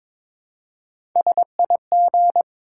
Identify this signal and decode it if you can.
Morse code, heard as a keyed tone, about 21 wpm, 700 Hz. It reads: SIG
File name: Top 90 QSO Elements - 2x Character Spacing - 22wpm